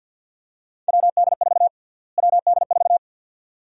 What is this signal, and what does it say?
Morse code, heard as a keyed tone, about 51 wpm, 700 Hz. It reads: WD4 WD4
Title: USA Call Sign Prefixes - Repeated 2x - 50wpm